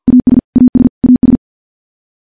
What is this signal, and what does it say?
Morse code, heard as a keyed tone, about 51 wpm, 250 Hz. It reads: AS AS AS
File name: morse / morse-training-50WPM